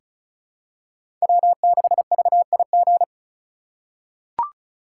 Morse code, heard as a keyed tone, about 36 wpm, 700 Hz. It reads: W6VIG
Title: USA Call Signs - 5 Characters - 8x Character Spacing - 35wpm